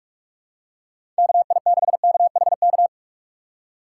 Morse code, heard as a keyed tone, about 46 wpm, 700 Hz. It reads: KI6KHK
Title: USA Call Signs - 6 Characters - 2x Character Spacing - 45wpm